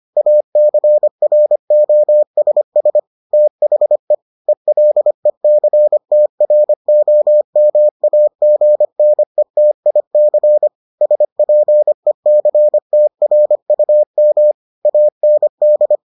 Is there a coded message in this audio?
Yes: Morse code from 0.2 to 16.0 s, heard as a keyed tone, about 25 wpm, 600 Hz. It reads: ACROSS THE ELECTROMAGNETIC SPECTRUM AND